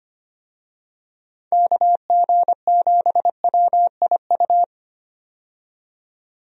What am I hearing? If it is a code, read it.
Morse code, heard as a keyed tone, about 25 wpm, 700 Hz. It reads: KG7WIU
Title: USA Call Signs - 6 Characters - 25wpm